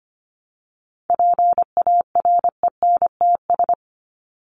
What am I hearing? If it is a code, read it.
Morse code, heard as a keyed tone, about 25 wpm, 700 Hz. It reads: PARENTS
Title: ES:S Suffix - Top 100 Words with Base Words 25wpm